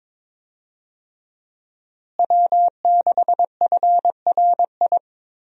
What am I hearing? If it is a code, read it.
Morse code, heard as a keyed tone, about 22 wpm, 700 Hz. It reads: W6FRI
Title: USA Call Signs - 5 Characters - Repeated 3x - 22wpm